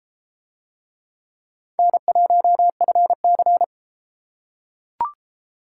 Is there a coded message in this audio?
Yes: Morse code from 1.8 to 3.6 s, heard as a keyed tone, about 32 wpm, 700 Hz. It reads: N1FC